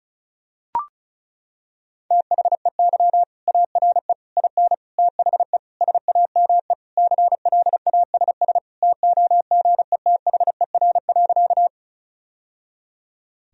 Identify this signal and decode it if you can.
Morse code, heard as a keyed tone, about 36 wpm, 700 Hz. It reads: THEY ARE IN THE SAME CLASS TOGETHER.
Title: Sentences from Top 1000 Words - Review Part 2 - 35wpm